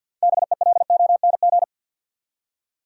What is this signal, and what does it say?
Morse code, heard as a keyed tone, about 47 wpm, 700 Hz. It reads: BELONG
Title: Six Letter Words - 50wpm